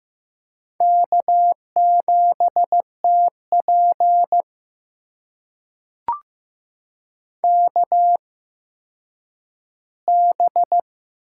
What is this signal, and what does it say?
Morse code, heard as a keyed tone, about 15 wpm, spaced as for 9 wpm, 700 Hz. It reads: K7TP KB
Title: USA Call Signs - 4 Characters - 8x Character Spacing - 15wpm